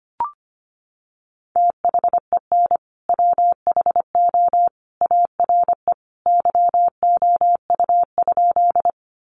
Morse code, heard as a keyed tone, about 25 wpm, 700 Hz. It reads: THEN WHO ARE YOU?
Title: Sentences from Top 100 Words 25wpm